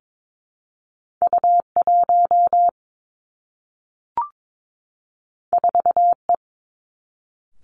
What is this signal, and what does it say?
Morse code, heard as a keyed tone, about 22 wpm, 700 Hz. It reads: U 1 4 E
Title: Two Challenging Letter-Number Combinations 22wpm